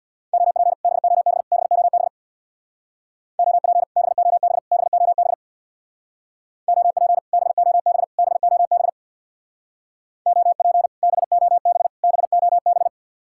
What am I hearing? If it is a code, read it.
Morse code, heard as a keyed tone, about 62 wpm, 700 Hz. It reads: OP BOB BOB OP BOB BOB OP BOB BOB OP BOB BOB
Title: Speed-Racing Common QSO Phrases Encore - 50wpm